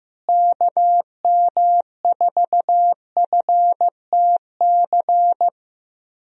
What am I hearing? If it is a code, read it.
Morse code, heard as a keyed tone, about 15 wpm, 700 Hz. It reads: KM4FTC
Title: USA Call Signs - 6 Characters - Repeated 3x - 15wpm